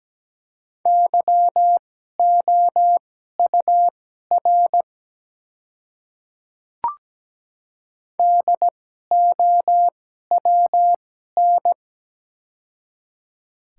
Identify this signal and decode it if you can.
Morse code, heard as a keyed tone, about 17 wpm, spaced as for 12 wpm, 700 Hz. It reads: YOUR DOWN
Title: 2x Character Spacing - Top 300 Words - 17wpm